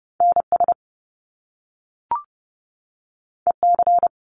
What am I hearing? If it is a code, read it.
Morse code, heard as a keyed tone, about 30 wpm, 700 Hz. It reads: NS EC
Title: Top Two Letter Combinations in Words 30wpm